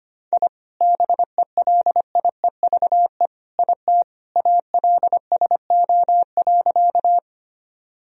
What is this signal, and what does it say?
Morse code, heard as a keyed tone, about 25 wpm, 700 Hz. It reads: I BELIEVE IT ALSO.
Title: Sentences from Top 1000 Words - Review Part 1 - 25wpm